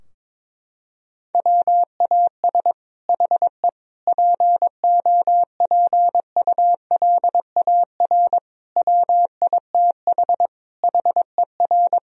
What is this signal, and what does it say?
Morse code, heard as a keyed tone, about 22 wpm, 700 Hz. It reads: WAS HE POPULAR WITH HER
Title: Sentences from Top 1000 Words - Review Part 3 - 22wpm